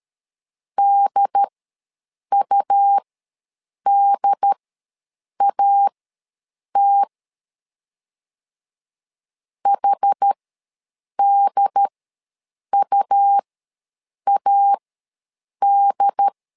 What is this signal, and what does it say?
Morse code, heard as a keyed tone, about 13 wpm, spaced as for 7 wpm, 800 Hz. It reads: DUDAT HDUAD
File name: LEZIONI / 2 UHDN / 2 UHD ws 12wpm 1722